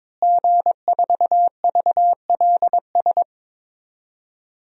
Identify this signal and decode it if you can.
Morse code, heard as a keyed tone, about 21 wpm, 700 Hz. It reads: G4VLS
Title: DXCC-100 Call Signs - 5 Characters - Repeated 2x - 22wpm